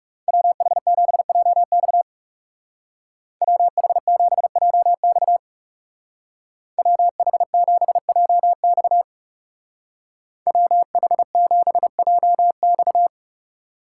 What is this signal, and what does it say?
Morse code, heard as a keyed tone, about 36 wpm, 700 Hz. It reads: WH7JX WH7JX WH7JX WH7JX
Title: Speed Racing - USA 2x2 Call Signs 30wpm